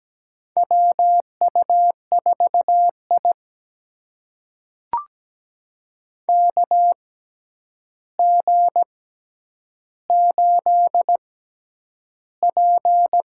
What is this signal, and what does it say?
Morse code, heard as a keyed tone, about 17 wpm, spaced as for 9 wpm, 700 Hz. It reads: WU4I KG8P
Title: USA Call Signs - 4 Characters - 6x Character Spacing - 17wpm